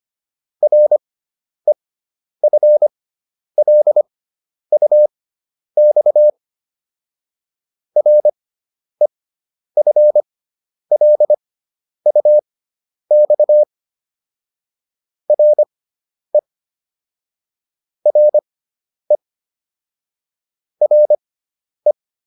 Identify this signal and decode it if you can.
Morse code, heard as a keyed tone, about 25 wpm, spaced as for 10 wpm, 600 Hz. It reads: REFLUX REFLUX RE RE RE